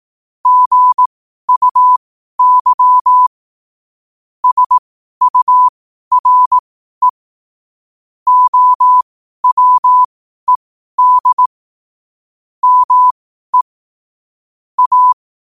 Morse code, heard as a keyed tone, about 18 wpm, spaced as for 12 wpm, 1000 Hz. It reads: GUY SURE OWED ME A
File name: qotdfiles / 12wpm